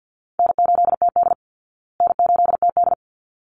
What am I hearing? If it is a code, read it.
Morse code, heard as a keyed tone, about 48 wpm, 700 Hz. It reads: N8TD N8TD
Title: Easy Call Signs Repeated 2x - 50wpm